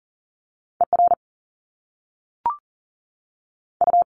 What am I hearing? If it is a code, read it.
Morse code, heard as a keyed tone, about 39 wpm, 700 Hz. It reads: ER U
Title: Top Two Letter Combinations in Words 40wpm